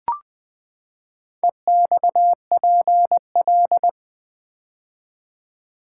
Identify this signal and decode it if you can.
Morse code, heard as a keyed tone, about 20 wpm, 700 Hz. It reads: EXPL
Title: Build A Word - 4 Letters At A Time - 11-Letter Words - 20wpm